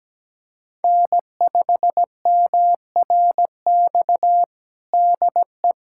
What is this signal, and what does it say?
Morse code, heard as a keyed tone, about 17 wpm, 700 Hz. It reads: N5MRX DE
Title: USA Callsign DE Callsign - 5 Characters - 17wpm